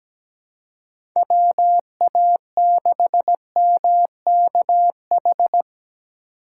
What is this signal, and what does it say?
Morse code, heard as a keyed tone, about 17 wpm, 700 Hz. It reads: WA6MKH
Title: USA Call Signs - 6 Characters - 3x Character Spacing - 17wpm